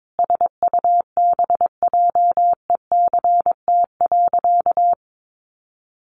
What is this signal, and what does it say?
Morse code, heard as a keyed tone, about 22 wpm, 700 Hz. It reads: SUBJECT.
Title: Sentences from Top 1400 Words 22wpm